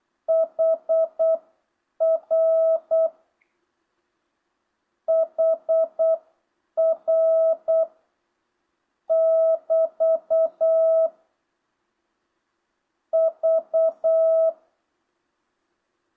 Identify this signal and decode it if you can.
Morse code, heard as a keyed tone, about 8 wpm, spaced as for 7 wpm, 1300 Hz, with noise. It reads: HR HR <BT> V